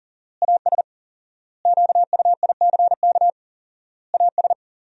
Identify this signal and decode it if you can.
Morse code, heard as a keyed tone, about 40 wpm, 700 Hz. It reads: AS QUICK AS
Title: 4x Word-Spacing - Sentences from Top 500 Words - 40wpm